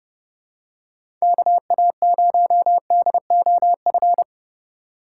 Morse code, heard as a keyed tone, about 32 wpm, 700 Hz. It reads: KA0DOF
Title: USA Call Signs - 6 Characters - 4x Character Spacing - 30wpm